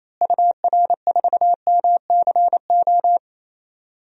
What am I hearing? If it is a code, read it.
Morse code, heard as a keyed tone, about 28 wpm, 700 Hz. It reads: UR4MCO
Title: DXCC-100 Call Signs - 6 Characters - Repeated 2x - 28wpm